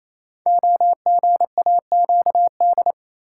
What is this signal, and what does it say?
Morse code, heard as a keyed tone, about 27 wpm, 700 Hz. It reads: OGAQD
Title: Code Groups - 5 Letters - 28wpm